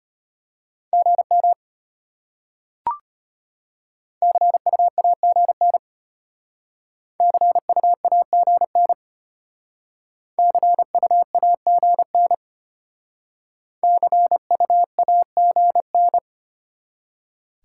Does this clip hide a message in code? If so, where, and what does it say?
Yes: Morse code from 0.9 to 16.2 s, heard as a keyed tone, about 32 wpm, 700 Hz. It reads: GM CUAGN CUAGN CUAGN CUAGN